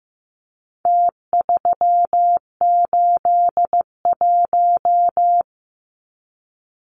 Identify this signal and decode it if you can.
Morse code, heard as a keyed tone, about 15 wpm, 700 Hz. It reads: T381
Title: Numbers - Contest Style - 4-Digit 15wpm